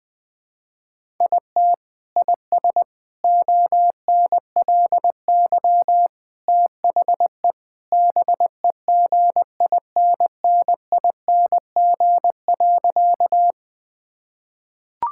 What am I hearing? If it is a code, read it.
Morse code, heard as a keyed tone, about 20 wpm, 700 Hz. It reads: IT IS ONLY THE BEGINNING.
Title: Sentences from Top 1000 Words - Review Part 5 - 20wpm